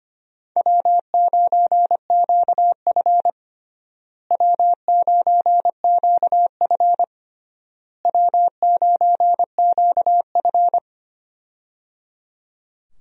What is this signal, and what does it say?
Morse code, heard as a keyed tone, about 25 wpm, 700 Hz. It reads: W9QF W9QF W9QF
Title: USA Call Signs - 4 Characters - Repeated 3x - 25wpm